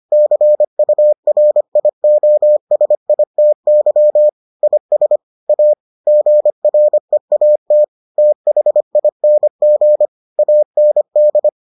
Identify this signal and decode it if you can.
Morse code, heard as a keyed tone, about 25 wpm, 600 Hz. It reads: CURIOSITY IS A GREAT THING AND